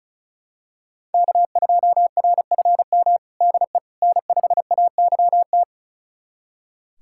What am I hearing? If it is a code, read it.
Morse code, heard as a keyed tone, about 34 wpm, 700 Hz. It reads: K2RFM DE N5AYT
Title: USA Callsign DE Callsign - 5 Characters - 35wpm